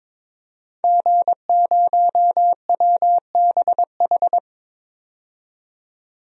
Morse code, heard as a keyed tone, about 22 wpm, 700 Hz. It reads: G0WBH
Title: DXCC-100 Call Signs - 5 Characters - 22wpm